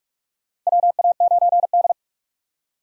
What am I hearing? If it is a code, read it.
Morse code, heard as a keyed tone, about 46 wpm, 700 Hz. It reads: WA9D
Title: USA Call Signs - 4 Characters - 45wpm